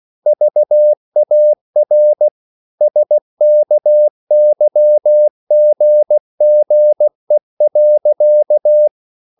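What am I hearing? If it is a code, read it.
Morse code, heard as a keyed tone, about 16 wpm, 600 Hz. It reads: VAR SKYGGE.
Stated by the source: Slow (QRS) Morse code at 16 wpm, from CW med Gnister.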